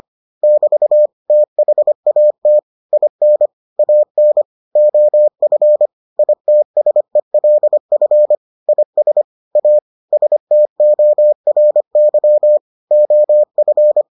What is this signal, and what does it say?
Morse code, heard as a keyed tone, about 25 wpm, 600 Hz. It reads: <BT> THAT IN AN OF ITSELF IS A STORY OF